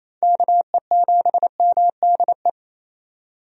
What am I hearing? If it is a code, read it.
Morse code, heard as a keyed tone, about 28 wpm, 700 Hz. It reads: KE7MDE
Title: USA Call Signs - 6 Characters - 28wpm